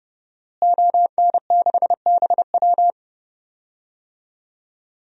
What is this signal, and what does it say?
Morse code, heard as a keyed tone, about 31 wpm, 700 Hz. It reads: ON6BW
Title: DXCC-100 Call Signs - 5 Characters - Repeated 2x - 30wpm